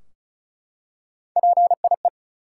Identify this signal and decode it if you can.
Morse code, heard as a keyed tone, about 36 wpm, 700 Hz. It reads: PIE
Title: Three Letter Words - 35wpm